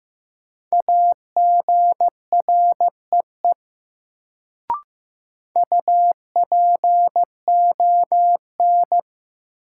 Morse code, heard as a keyed tone, about 16 wpm, 700 Hz. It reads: AGREE UPON
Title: Sentences from Top 1000 Words - Review Part 6 - New Word Drill - 15wpm